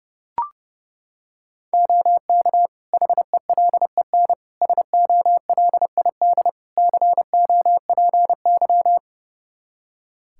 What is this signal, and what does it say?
Morse code, heard as a keyed tone, about 30 wpm, 700 Hz. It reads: OK HELEN SOLID COPY
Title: Practice QSO - Exchange 3 - More Info - 30wpm